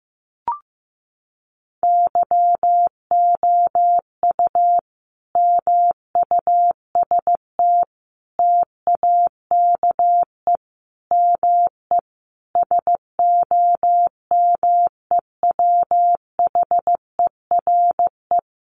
Morse code, heard as a keyed tone, about 15 wpm, 700 Hz. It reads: YOU MUST TAKE ME SOMEWHERE
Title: Sentences from Top 1400 Words 15wpm